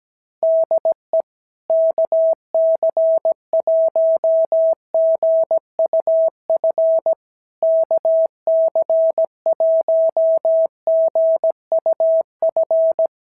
Morse code, heard as a keyed tone, about 17 wpm, 650 Hz. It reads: DE KC1GUF KC1GUF
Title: SOTA QSO Practice 17wpm